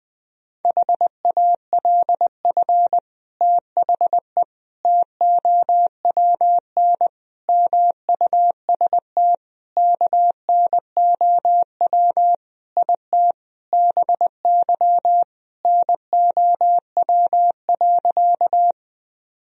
Morse code, heard as a keyed tone, about 20 wpm, 700 Hz. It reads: HALF THE TOWN MUST KNOW IT BY NOW.
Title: Sentences from Top 1000 Words - Review Part 2 - 20wpm